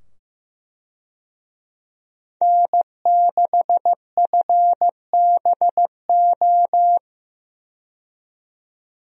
Morse code, heard as a keyed tone, about 15 wpm, 700 Hz. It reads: N6FBO
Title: USA Call Signs - 5 Characters - 15wpm